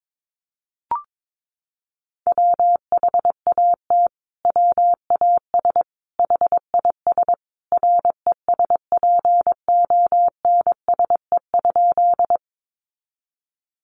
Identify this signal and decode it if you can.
Morse code, heard as a keyed tone, about 22 wpm, 700 Hz. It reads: WHAT WAS HIS RESPONSE?
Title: Sentences from Top 1400 Words 22wpm